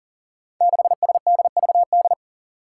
Morse code, heard as a keyed tone, about 41 wpm, 690 Hz. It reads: 6SDVD
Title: Five Challenging Letter-Number Combinations 40wpm